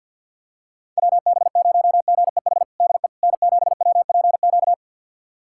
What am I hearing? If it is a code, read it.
Morse code, heard as a keyed tone, about 51 wpm, 700 Hz. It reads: WB0GEH DE N7WPQ
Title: USA Callsign DE Callsign 50wpm